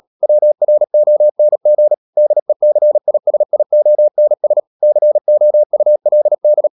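Morse code, heard as a keyed tone, about 37 wpm, 600 Hz. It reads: WRONG DECISIONS COULD
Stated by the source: Morse code practice at 35–39 wpm sending English text of Ken Follett's novel Never, from CW med Gnister.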